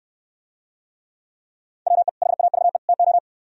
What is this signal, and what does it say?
Morse code, heard as a keyed tone, about 69 wpm, 700 Hz. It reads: WE HAVE TO